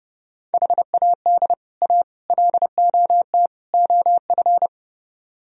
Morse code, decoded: HAD A LOT OF

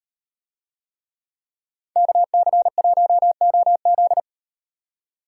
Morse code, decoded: KC1OZ

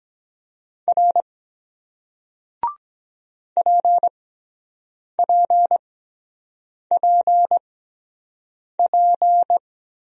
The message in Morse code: R PPPP

